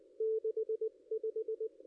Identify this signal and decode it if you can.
Morse code, heard as a keyed tone, about 20 wpm, 440 Hz. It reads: 65
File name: contests / ss-cw-k5tr-op-n5rz -2200